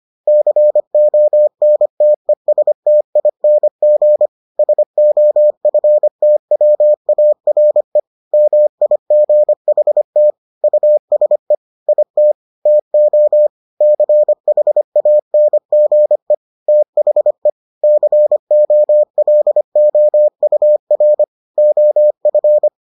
Morse code, heard as a keyed tone, about 25 wpm, 600 Hz. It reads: CONTESTING SOFTWARE MIGHT USE IT TO CHANGE THE COLOUR OF